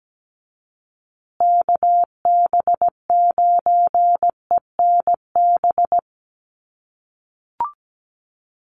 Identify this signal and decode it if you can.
Morse code, heard as a keyed tone, about 17 wpm, 700 Hz. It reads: KB9ENB
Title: USA Call Signs Repeated 3x 17wpm